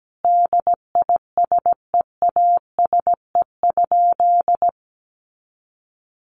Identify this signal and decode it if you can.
Morse code, heard as a keyed tone, about 17 wpm, 700 Hz. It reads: DISEASE?
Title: Sentences from Top 1200 Words 17wpm